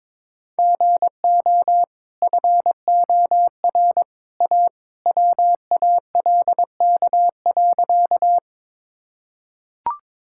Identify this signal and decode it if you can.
Morse code, heard as a keyed tone, about 22 wpm, 690 Hz. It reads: GO FOR A WALK.
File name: practice / 22wpm / Easy Sentences with Hints - 22wpm